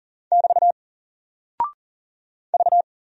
Morse code, heard as a keyed tone, about 40 wpm, 700 Hz. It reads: <BT> V
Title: Single Character (Letter:Number:Punctuation:Prosign) Alt 40wpm